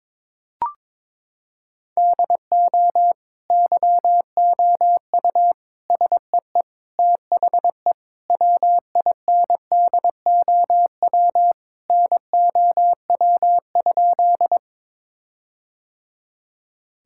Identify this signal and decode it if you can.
Morse code, heard as a keyed tone, about 21 wpm, 700 Hz. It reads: DO YOU SEE THE WINDOW NOW?
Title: Sentences from Top 1600 Words 22wpm